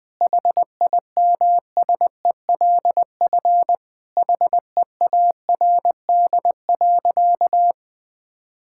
Morse code, heard as a keyed tone, about 20 wpm, 700 Hz. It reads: HIMSELF HEARD.
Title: Sentences from Top 1600 Words 20wpm